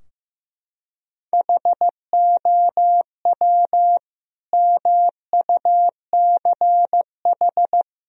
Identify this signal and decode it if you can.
Morse code, heard as a keyed tone, about 15 wpm, 700 Hz. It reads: HOW MUCH
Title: Sentences from Top 1000 Words - Review Part 3 - 15wpm